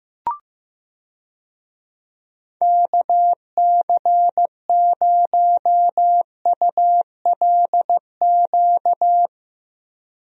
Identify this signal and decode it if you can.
Morse code, heard as a keyed tone, about 15 wpm, 700 Hz. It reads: KC0ULQ
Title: USA Call Signs - 6 Characters - Repeated 3x - 15wpm